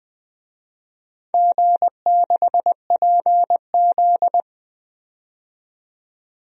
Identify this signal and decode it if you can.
Morse code, heard as a keyed tone, about 20 wpm, 700 Hz. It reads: G6PZ
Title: DXCC-100 Call Signs - 4 Characters - 20wpm